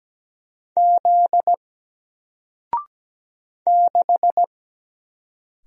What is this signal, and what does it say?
Morse code, heard as a keyed tone, about 17 wpm, 700 Hz. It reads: Z 6
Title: Single Character (Letter:Number:Punctuation:Prosign) Alt 17wpm